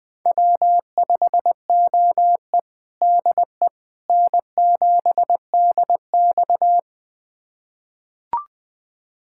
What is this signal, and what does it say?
Morse code, heard as a keyed tone, about 21 wpm, 700 Hz. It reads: W5OE DE N7DX
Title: USA Callsign DE Callsign - 4 Characters - 20wpm